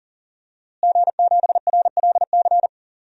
Morse code, heard as a keyed tone, about 40 wpm, 700 Hz. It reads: G7RLC
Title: DXCC-100 Call Signs - 5 Characters - Repeated 2x - 40wpm